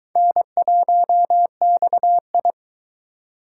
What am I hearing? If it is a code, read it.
Morse code, heard as a keyed tone, about 22 wpm, 700 Hz. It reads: N1XI